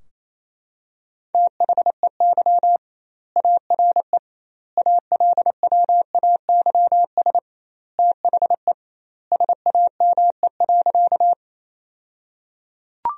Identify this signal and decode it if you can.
Morse code, heard as a keyed tone, about 28 wpm, 700 Hz. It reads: THEY ARE ALWAYS THE SAME.
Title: 2x Word-Spacing - Sentences from Top 300 Words - 28wpm